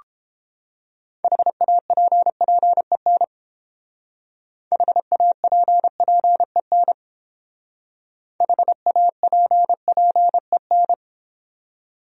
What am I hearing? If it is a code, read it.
Morse code, heard as a keyed tone, about 31 wpm, 700 Hz. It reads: HAPPEN HAPPEN HAPPEN